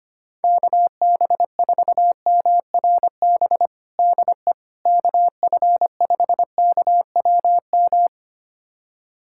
Morse code, heard as a keyed tone, about 25 wpm, 700 Hz. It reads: KB4MRB DE KF5KWM
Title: USA Callsign DE Callsign 25wpm